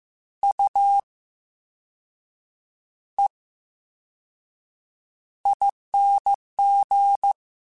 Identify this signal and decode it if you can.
Morse code, heard as a keyed tone, about 15 wpm, spaced as for 6 wpm, 800 Hz. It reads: UEING
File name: Special INGs-mp3s / Special INGs-mp3s / VerySlowToFaster Ing @13